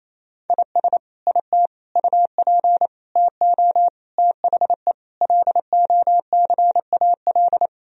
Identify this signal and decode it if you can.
Morse code, heard as a keyed tone, about 28 wpm, 700 Hz. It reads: IS IT UP TO THE LOCAL
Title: Sentences from Top 1000 Words - Review Part 4 - 28wpm